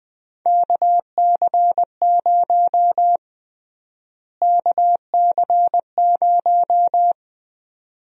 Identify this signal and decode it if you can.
Morse code, heard as a keyed tone, about 20 wpm, 700 Hz. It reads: KC0 KC0
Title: USA Call Sign Prefixes - Repeated 2x - 20wpm